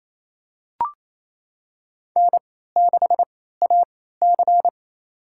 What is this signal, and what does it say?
Morse code, heard as a keyed tone, about 28 wpm, spaced as for 16 wpm, 700 Hz. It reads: N6AC
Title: USA Call Signs - 4 Characters - 3x Character Spacing - 28wpm